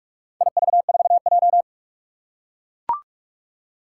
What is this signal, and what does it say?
Morse code, heard as a keyed tone, about 44 wpm, 700 Hz. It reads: IV4J
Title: Four Challenging Letter-Number Combinations 45wpm